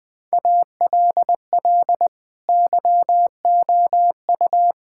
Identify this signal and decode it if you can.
Morse code, heard as a keyed tone, about 20 wpm, 700 Hz. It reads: ALL YOU
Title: Sentences with 9-Word Phrases - 20wpm